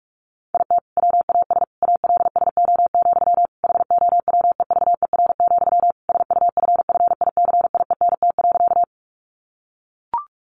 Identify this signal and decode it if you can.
Morse code, heard as a keyed tone, about 44 wpm, 700 Hz. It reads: IT WAS ALSO, HOWEVER, SUFFICIENT.
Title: Sentences from Top 1400 Words 45wpm